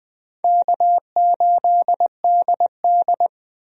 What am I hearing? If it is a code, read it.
Morse code, heard as a keyed tone, about 20 wpm, 700 Hz. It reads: K8DD
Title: USA Call Signs - 4 Characters - Repeated 3x - 20wpm